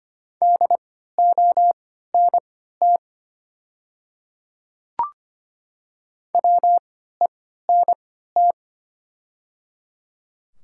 Morse code, heard as a keyed tone, about 25 wpm, spaced as for 14 wpm, 700 Hz. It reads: DONT WENT
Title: 3x Character Spacing - Top 300 Words - 25wpm